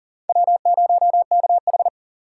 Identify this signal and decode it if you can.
Morse code, heard as a keyed tone, about 40 wpm, 700 Hz. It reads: W0KH
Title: USA Call Signs - 4 Characters - 4x Character Spacing - 40wpm